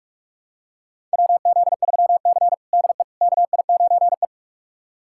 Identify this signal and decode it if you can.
Morse code, heard as a keyed tone, about 41 wpm, 700 Hz. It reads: WZ3C DE KI9E
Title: USA Callsign DE Callsign - 4 Characters - 45wpm